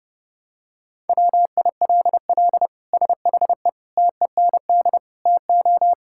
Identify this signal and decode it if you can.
Morse code, heard as a keyed tone, about 30 wpm, 700 Hz. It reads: WILL SHE TEND TO
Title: Sentences from Top 1000 Words - Review Part 5 - 30wpm